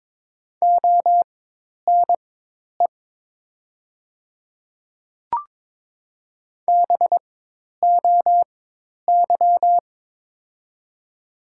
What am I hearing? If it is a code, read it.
Morse code, heard as a keyed tone, about 21 wpm, spaced as for 10 wpm, 700 Hz. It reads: ONE BOY
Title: 4x Character Spacing - Top 300 Words - 22wpm